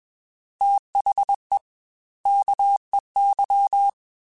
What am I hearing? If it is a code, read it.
Morse code, heard as a keyed tone, about 21 wpm, 800 Hz. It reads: THE KEY
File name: Technical Subjects / Technical Subjects-mp3s / Reactance @20